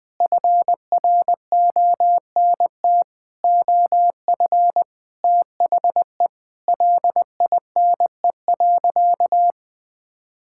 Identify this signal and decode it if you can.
Morse code, heard as a keyed tone, about 20 wpm, 700 Hz. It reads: FRONT OF THE LINE.
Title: Sentences with 5-Word Phrases - Spoken Beginning - 20wpm